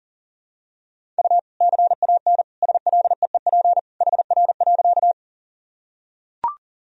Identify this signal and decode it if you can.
Morse code, heard as a keyed tone, about 40 wpm, 700 Hz. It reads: U CAN SLEEP HR.
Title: Sentences from Top 500 Words and Abbreviations 40wpm